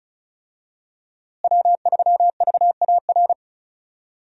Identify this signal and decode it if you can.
Morse code, heard as a keyed tone, about 36 wpm, 700 Hz. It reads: W3VAR
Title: USA Call Signs - 5 Characters - Repeated 3x - 35wpm